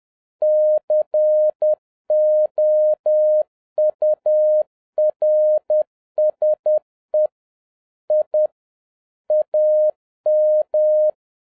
Morse code, heard as a keyed tone, about 10 wpm, 610 Hz. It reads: COURSE I AM